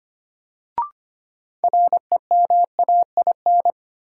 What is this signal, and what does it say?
Morse code, heard as a keyed tone, about 25 wpm, 700 Hz. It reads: REMAIN
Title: Sentences from Top 1200 Words - New Word Drill Alt - 25wpm